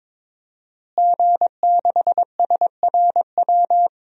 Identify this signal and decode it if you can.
Morse code, heard as a keyed tone, about 21 wpm, 700 Hz. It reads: G6SRW
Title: DXCC-100 Call Signs - 5 Characters - Repeated 2x - 22wpm